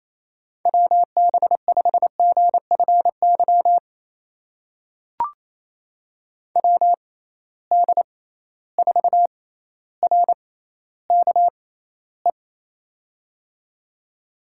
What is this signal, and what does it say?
Morse code, heard as a keyed tone, about 28 wpm, spaced as for 14 wpm, 700 Hz. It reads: WB5GFY WD4RKE
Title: USA Call Signs - 6 Characters - 6x Character Spacing - 28wpm